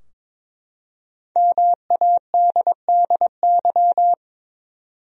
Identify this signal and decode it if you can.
Morse code, heard as a keyed tone, about 22 wpm, 700 Hz. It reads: MADDY
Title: USA Common Names - Max 5 Letters - 22wpm